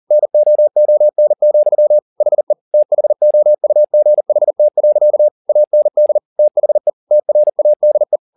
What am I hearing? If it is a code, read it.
Morse code, heard as a keyed tone, about 40 wpm, 600 Hz. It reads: NOON, HE THOUGHT. AND THE TRADE
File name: the old man and the sea 2 40wpm